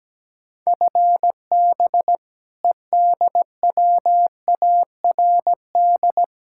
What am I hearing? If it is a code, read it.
Morse code, heard as a keyed tone, about 17 wpm, 700 Hz. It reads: FB EDWARD
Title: Practice QSO - Exchange 3 - More Info - 17wpm